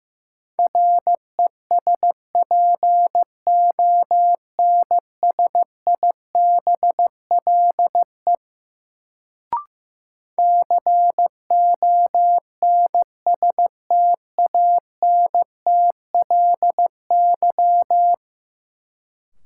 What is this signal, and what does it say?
Morse code, heard as a keyed tone, about 16 wpm, 700 Hz. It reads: RESPONSIBLE CONSTANTLY
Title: Sentences from Top 1000 Words - Review Part 7 - New Word Drill - 15wpm